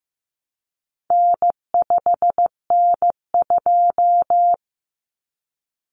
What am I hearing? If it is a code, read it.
Morse code, heard as a keyed tone, about 15 wpm, 700 Hz. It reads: N5N2
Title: Common Cut Numbers - 4-Digit 15wpm